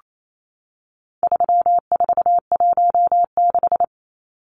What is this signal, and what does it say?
Morse code, heard as a keyed tone, about 28 wpm, 700 Hz. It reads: 3416
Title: Numbers - 4-Digit 28wpm